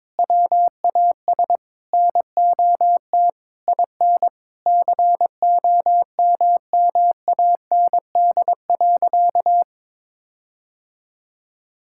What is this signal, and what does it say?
Morse code, heard as a keyed tone, about 22 wpm, 700 Hz. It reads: WAS NOT IN COMMAND.
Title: Sentences from Top 1600 Words 22wpm